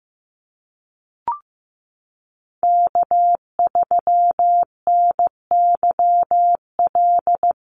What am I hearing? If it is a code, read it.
Morse code, heard as a keyed tone, about 15 wpm, 700 Hz. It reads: K3NYL
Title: USA Call Signs Repeated 2x 15wpm